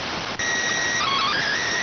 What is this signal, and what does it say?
SSTV (slow-scan television), PD 290.